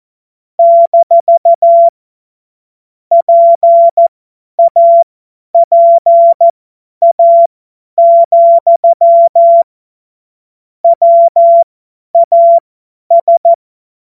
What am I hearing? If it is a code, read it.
Morse code, heard as a keyed tone, about 14 wpm, spaced as for 10 wpm, 680 Hz. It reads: - PAPA, WAS
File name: Morsekurs / Witze / Witz 02. Februar 2021, 50 F70